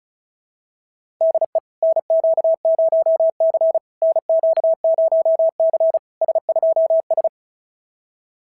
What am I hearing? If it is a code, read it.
Morse code, heard as a keyed tone, about 33 wpm, 650 Hz. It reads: DE NQ0C NQ0C S2S
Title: SOTA QSO Practice 35wpm